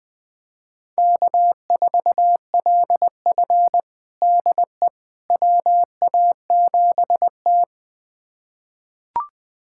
Morse code, heard as a keyed tone, about 21 wpm, 700 Hz. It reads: K4LF DE WA7T
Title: USA Callsign DE Callsign - 4 Characters - 20wpm